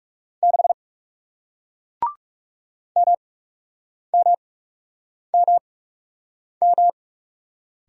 Morse code, heard as a keyed tone, about 39 wpm, spaced as for 8 wpm, 700 Hz. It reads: 6 MMMM